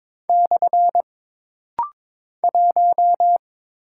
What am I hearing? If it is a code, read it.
Morse code, heard as a keyed tone, about 22 wpm, 700 Hz. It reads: / 1
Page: This rapid-fire timing is geared to push your ICR (Instant Character Recognition) skills.